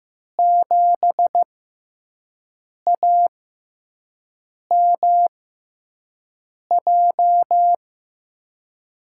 Morse code, heard as a keyed tone, about 15 wpm, spaced as for 5 wpm, 700 Hz. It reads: 7AMJ
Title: USA Call Signs - 6 Characters - 6x Character Spacing - 15wpm